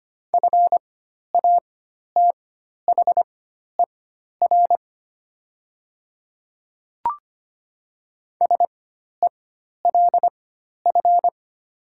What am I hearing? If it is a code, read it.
Morse code, heard as a keyed tone, about 25 wpm, spaced as for 12 wpm, 700 Hz. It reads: FATHER SELF
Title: 4x Character Spacing - Top 300 Words - 25wpm